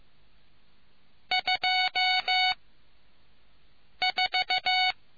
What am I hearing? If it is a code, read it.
Morse code, heard as a keyed tone, about 15 wpm, spaced as for 5 wpm, 740 Hz. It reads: 24